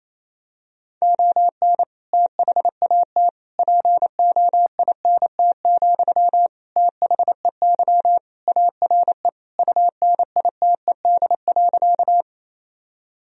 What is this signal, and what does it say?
Morse code, heard as a keyed tone, about 28 wpm, 700 Hz. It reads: ON THAT POINT, THEY ARE UNITED.
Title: Sentences from Top 1000 Words - Review Part 4 - 28wpm